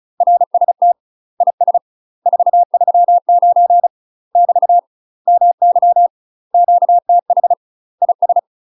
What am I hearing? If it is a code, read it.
Morse code, heard as a keyed tone, about 34 wpm, 700 Hz. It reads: RST IS 439 <BT> MY QTH IS
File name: UK MASTER QSOs TRAINING/35WPM